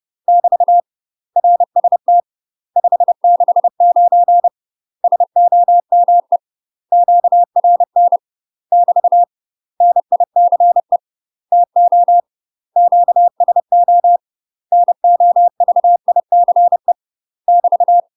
Morse code, heard as a keyed tone, about 30 wpm, 700 Hz. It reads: <BT> RST 569 SOME QRN <BT> NICE TO QSO NOVICE <BT>
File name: UK MASTER QSOs TRAINING / 30WPM / UK MASTER QSOs 0011